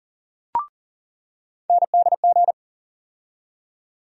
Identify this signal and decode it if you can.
Morse code, heard as a keyed tone, about 40 wpm, 700 Hz. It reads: NDG
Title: USA Call Sign Postfixes - 3 Letters 40wpm